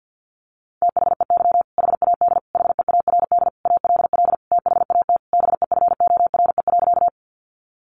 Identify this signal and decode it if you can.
Morse code, heard as a keyed tone, about 50 wpm, 700 Hz. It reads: THEY HAD HEARD ALL THAT BEFORE.
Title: Sentences from Top 400 Words 50wpm